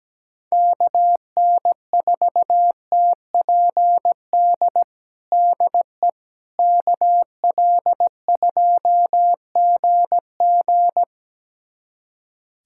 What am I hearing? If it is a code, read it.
Morse code, heard as a keyed tone, about 17 wpm, 700 Hz. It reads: KN4TPD DE KL2GG
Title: USA Callsign DE Callsign 17wpm